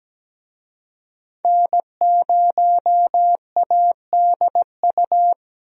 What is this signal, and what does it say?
Morse code, heard as a keyed tone, about 17 wpm, 700 Hz. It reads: N0ADU
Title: USA Call Signs - 5 Characters - Repeated 2x - 17wpm